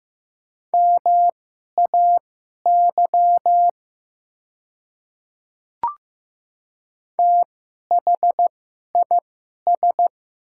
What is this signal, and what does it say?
Morse code, heard as a keyed tone, about 15 wpm, spaced as for 11 wpm, 700 Hz. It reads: MAY THIS
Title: 2x Character Spacing - Top 100 Words - 15wpm